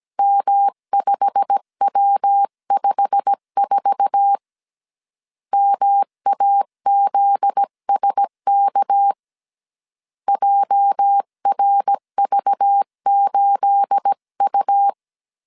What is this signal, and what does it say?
Morse code, heard as a keyed tone, about 17 wpm, 800 Hz. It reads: M5W54 MAZSK JRV8U
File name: 1500C S B 16wpm